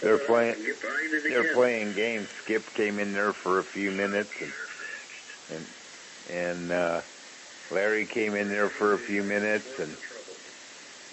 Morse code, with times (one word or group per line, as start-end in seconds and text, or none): none